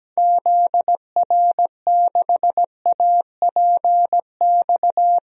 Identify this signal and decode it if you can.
Morse code, heard as a keyed tone, about 17 wpm, 700 Hz. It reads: ZR6APX
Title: DXCC-100 Call Signs - 6 Characters - Repeated 2x - 17wpm